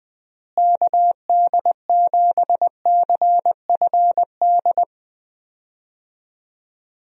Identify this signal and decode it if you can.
Morse code, heard as a keyed tone, about 20 wpm, 700 Hz. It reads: KD7CFD
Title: USA Call Signs - 6 Characters - Repeated 3x - 20wpm